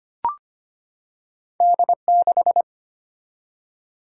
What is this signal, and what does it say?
Morse code, heard as a keyed tone, about 25 wpm, 690 Hz. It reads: D6
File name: Two Letter-Number Combinations - 25wpm